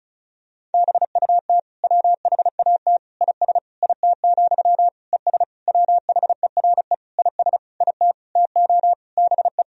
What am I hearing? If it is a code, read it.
Morse code, heard as a keyed tone, about 36 wpm, 700 Hz. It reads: BUT WHAT IS IT, ES WHERE IS IT TO BE
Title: Sentences from Top 300 Words and Abbreviations 35wpm